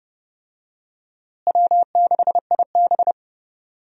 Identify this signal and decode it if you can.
Morse code, heard as a keyed tone, about 30 wpm, 700 Hz. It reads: W6IB
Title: USA Call Signs - 4 Characters - Repeated 3x - 30wpm